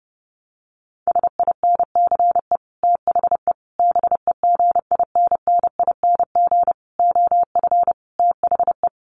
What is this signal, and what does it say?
Morse code, heard as a keyed tone, about 30 wpm, 700 Hz. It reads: SINCE THE BEGINNING OF THE
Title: Sets of 5 Words Encore2 30wpm